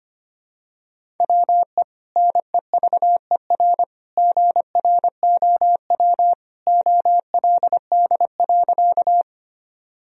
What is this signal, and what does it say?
Morse code, heard as a keyed tone, about 25 wpm, 700 Hz. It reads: WE NEVER GROW OLD.
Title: Sentences from Top 1000 Words - Review Part 1 - 25wpm